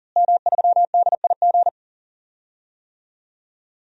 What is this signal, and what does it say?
Morse code, heard as a keyed tone, about 40 wpm, 700 Hz. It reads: M3DIG
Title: DXCC-100 Call Signs - 5 Characters - Repeated 3x - 40wpm